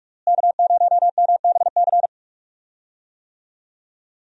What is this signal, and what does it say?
Morse code, heard as a keyed tone, about 46 wpm, 700 Hz. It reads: K0MBC
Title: USA Call Signs - 5 Characters - Repeated 3x - 45wpm